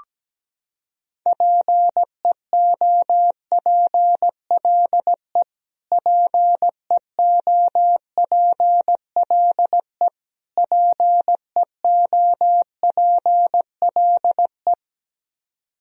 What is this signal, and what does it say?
Morse code, heard as a keyed tone, about 17 wpm, 700 Hz. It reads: PEOPLE PEOPLE PEOPLE
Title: Top 100 Words Repeated 17wpm